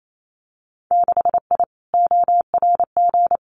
Morse code, heard as a keyed tone, about 28 wpm, 700 Hz. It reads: 6I ORG
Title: Winter Field Day Exchanges Encore 28wpm